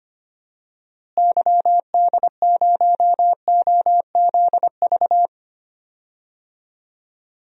Morse code, heard as a keyed tone, about 25 wpm, 700 Hz. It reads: YD0OZV
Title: DXCC-100 Call Signs - 6 Characters - Repeated 2x - 25wpm